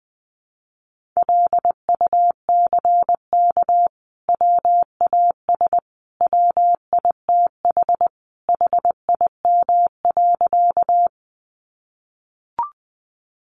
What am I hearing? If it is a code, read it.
Morse code, heard as a keyed tone, about 20 wpm, 700 Hz. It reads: LUCK WAS WITH HIM.
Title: Sentences from Top 1500 Words 20wpm